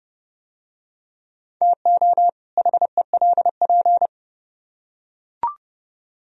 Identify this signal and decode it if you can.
Morse code, heard as a keyed tone, about 29 wpm, 700 Hz. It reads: TO HELP